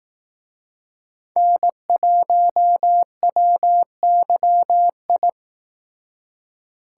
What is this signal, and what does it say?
Morse code, heard as a keyed tone, about 18 wpm, 700 Hz. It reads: N1WYI
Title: Speed Racing - USA Call Signs 15wpm